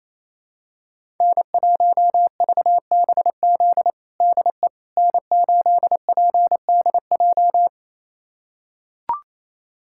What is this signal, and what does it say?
Morse code, heard as a keyed tone, about 28 wpm, 700 Hz. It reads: N1VBZ DE N8PDJ
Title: USA Callsign DE Callsign - 5 Characters - 28wpm